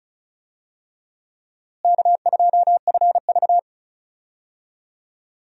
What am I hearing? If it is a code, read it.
Morse code, heard as a keyed tone, about 34 wpm, 700 Hz. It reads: K2FV
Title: USA Call Signs - 4 Characters - Repeated 3x - 35wpm